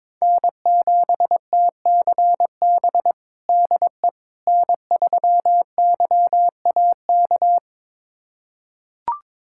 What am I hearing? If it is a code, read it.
Morse code, heard as a keyed tone, about 22 wpm, 700 Hz. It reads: N7TCB DE N3YAK
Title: USA Callsign DE Callsign - 5 Characters - 22wpm